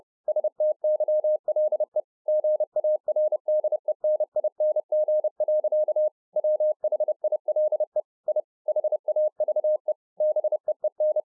Morse code, heard as a keyed tone, about 30 wpm, 610 Hz. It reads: STYLE GARDENING. WHILE I HAVE BEEN